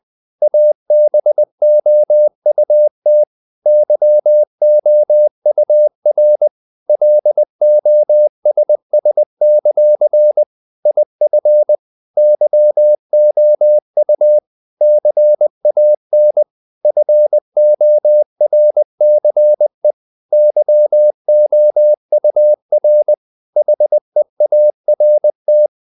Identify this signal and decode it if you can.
Morse code, heard as a keyed tone, about 20 wpm, 600 Hz. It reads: ABOUT YOUR LOSS; IF YOU CAN FORCE YOUR HEART
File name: CW / Practice / If20